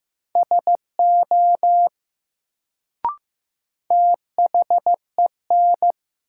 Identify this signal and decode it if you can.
Morse code, heard as a keyed tone, about 15 wpm, 700 Hz. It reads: SO THEN
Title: Sets of 2 Words - New Word Drill - 15wpm